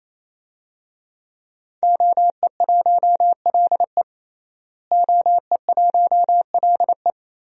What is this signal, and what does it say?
Morse code, heard as a keyed tone, about 27 wpm, 700 Hz. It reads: OE1LE OE1LE
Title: DXCC-100 Call Signs - 5 Characters - Repeated 3x - 28wpm